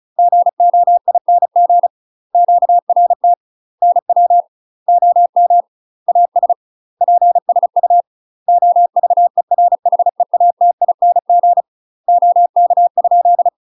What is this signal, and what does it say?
Morse code, heard as a keyed tone, about 33 wpm, 700 Hz. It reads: GOING QRT NW OM AS PSU OVERHEATING OK?
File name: UK MASTER QSOs TRAINING / 35WPM / UK MASTER QSOs 0013